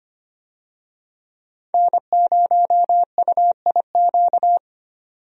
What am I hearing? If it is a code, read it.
Morse code, heard as a keyed tone, about 25 wpm, 700 Hz. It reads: N0UIQ
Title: USA Call Signs - 5 Characters - 25wpm